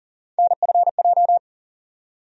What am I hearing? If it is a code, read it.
Morse code, heard as a keyed tone, about 40 wpm, 700 Hz. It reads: NFJ
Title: USA Call Sign Postfixes - 3 Letters 40wpm